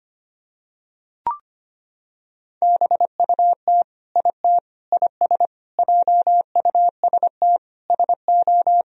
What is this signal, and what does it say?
Morse code, heard as a keyed tone, about 25 wpm, 700 Hz. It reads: BUT IT IS JUST SO
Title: Sentences from Top 200 Words and Abbreviations 25wpm